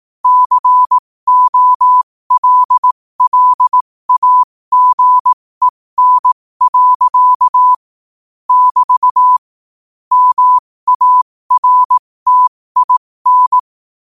Morse code, heard as a keyed tone, about 18 wpm, spaced as for 15 wpm, 1000 Hz. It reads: COLLAGEN. <BT> MARTIN
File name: qotdfiles / 15wpm